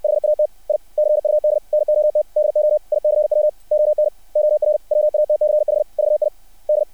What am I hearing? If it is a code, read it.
Morse code, heard as a keyed tone, about 60 wpm, 600 Hz. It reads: HAT I CUD NOT DO ICW ON OR WATCH VID
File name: archived / CRSnet